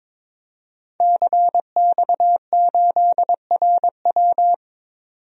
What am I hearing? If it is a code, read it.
Morse code, heard as a keyed tone, about 22 wpm, 700 Hz. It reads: CX8RW
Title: DXCC-100 Call Signs - 5 Characters - 22wpm